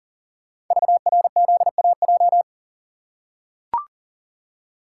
Morse code, heard as a keyed tone, about 40 wpm, 690 Hz. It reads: VRZAJ
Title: Code Groups - 5 Letters - 40wpm